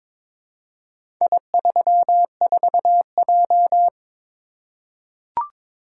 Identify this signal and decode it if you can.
Morse code, heard as a keyed tone, about 22 wpm, 700 Hz. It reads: I34J
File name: practice / 22wpm / Four Letter-Number Combinations - 22wpm